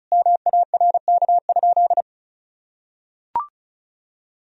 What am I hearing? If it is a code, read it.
Morse code, heard as a keyed tone, about 34 wpm, 700 Hz. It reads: MARK?
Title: Sentences from Top 1000 Words - Review Part 1 - 35wpm